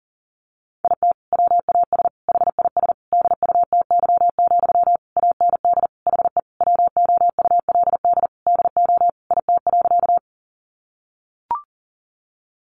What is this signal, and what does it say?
Morse code, heard as a keyed tone, about 40 wpm, 700 Hz. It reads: IT WAS HIS DUTY, AND HE WOULD DO IT.
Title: Sentences from Top 1500 Words 40wpm